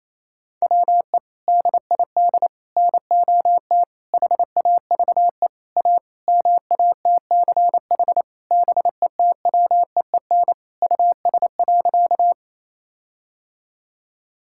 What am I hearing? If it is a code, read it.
Morse code, heard as a keyed tone, about 28 wpm, 700 Hz. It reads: WE DID NOT HAVE A MATCH BETWEEN US.
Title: Sentences from Top 1600 Words 28wpm